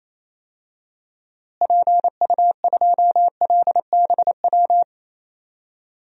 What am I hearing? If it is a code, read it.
Morse code, heard as a keyed tone, about 28 wpm, 700 Hz. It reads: PU2LBW
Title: DXCC-100 Call Signs - 6 Characters - Repeated 3x - 28wpm